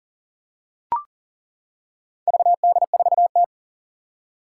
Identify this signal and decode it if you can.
Morse code, heard as a keyed tone, about 40 wpm, 700 Hz. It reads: VD4T
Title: Four Challenging Letter-Number Combinations 40wpm